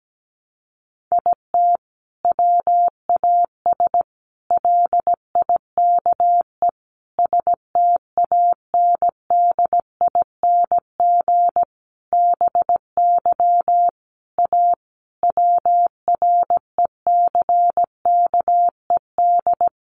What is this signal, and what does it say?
Morse code, heard as a keyed tone, about 17 wpm, 700 Hz. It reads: IT WAS LIKE STANDING BY A WRECKED
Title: A Sound Of Thunder 17wpm